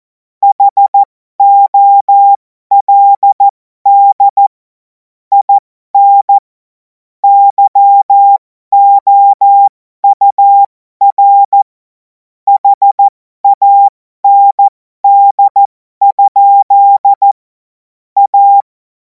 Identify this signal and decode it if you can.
Morse code, heard as a keyed tone, about 14 wpm, spaced as for 12 wpm, 800 Hz. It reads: HOLD IN YOUR HAND? A
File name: Morsekurs / Witze